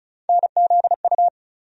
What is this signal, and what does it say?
Morse code, heard as a keyed tone, about 34 wpm, 700 Hz. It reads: NZU
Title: USA Call Sign Postfixes - 3 Letters 35wpm